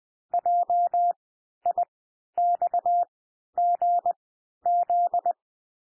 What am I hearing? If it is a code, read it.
Morse code, heard as a keyed tone, about 20 wpm, spaced as for 11 wpm, 700 Hz. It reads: JIXGZ
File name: cwmp3 / L1 SOUND LETTERS 20 wpm